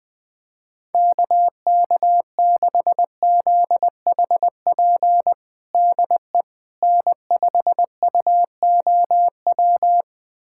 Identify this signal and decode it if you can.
Morse code, heard as a keyed tone, about 20 wpm, 700 Hz. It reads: KK6ZHP DE N5UOW
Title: USA Callsign DE Callsign 20wpm